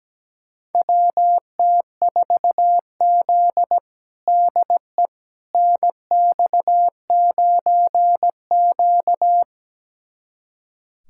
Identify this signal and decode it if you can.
Morse code, heard as a keyed tone, about 17 wpm, 700 Hz. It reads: WT4Z DE NX9Q
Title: USA Callsign DE Callsign - 4 Characters - 17wpm